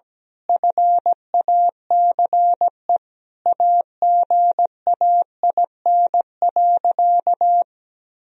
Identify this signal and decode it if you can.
Morse code, heard as a keyed tone, about 17 wpm, 700 Hz. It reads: FACE AGAIN.
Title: Sentences from Top 1000 Words - Review Part 2 - 17wpm